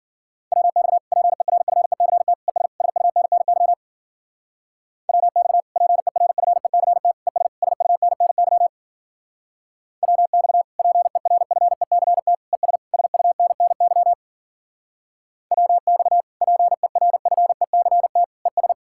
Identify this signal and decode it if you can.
Morse code, heard as a keyed tone, about 49 wpm, 700 Hz. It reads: WX PERFECT ES SUNNY WX PERFECT ES SUNNY WX PERFECT ES SUNNY WX PERFECT ES
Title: Speed-Racing Common QSO Phrases - 40wpm